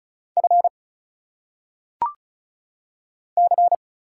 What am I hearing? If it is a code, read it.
Morse code, heard as a keyed tone, about 34 wpm, 700 Hz. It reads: F C
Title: Single Letters - Phonetic Pronunciation 35wpm